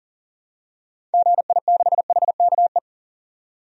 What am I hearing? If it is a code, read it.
Morse code, heard as a keyed tone, about 40 wpm, 700 Hz. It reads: GI6HKE
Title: DXCC-100 Call Signs - 6 Characters - 40wpm